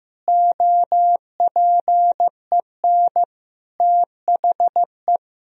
Morse code, heard as a keyed tone, about 16 wpm, 700 Hz. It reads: OPEN THE
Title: Sentences from Top 1000 Words - Review Part 3 - 15wpm